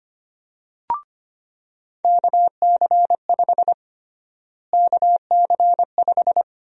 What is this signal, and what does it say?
Morse code, heard as a keyed tone, about 25 wpm, 700 Hz. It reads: KC5 KC5
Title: USA Call Sign Prefixes - Repeated 2x - 25wpm